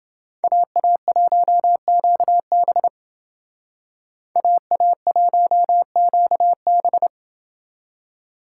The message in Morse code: AA1QB AA1QB

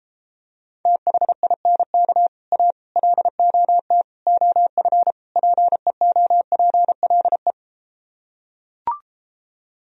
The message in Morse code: THINK A LOT OF PEOPLE

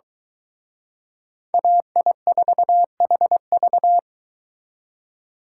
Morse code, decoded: AI4HV

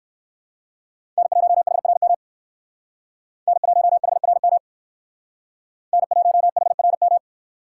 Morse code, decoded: N1HRM N1HRM N1HRM

